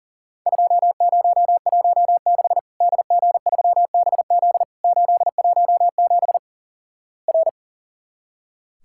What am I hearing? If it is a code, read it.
Morse code, heard as a keyed tone, about 37 wpm, 700 Hz. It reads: 2016 DG3BZ 817 R
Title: Worked All Europe DX Contest - QTC Exchange - 40wpm